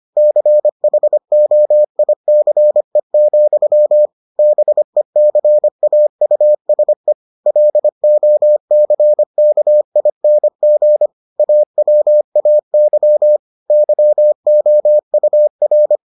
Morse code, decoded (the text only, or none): CHOICE, BECAUSE LOCKING AWAY YOUR